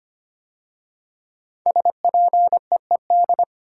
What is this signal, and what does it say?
Morse code, heard as a keyed tone, about 25 wpm, 700 Hz. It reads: SPEED
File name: 5 Letter Words - First 3 Letters Given - 25wpm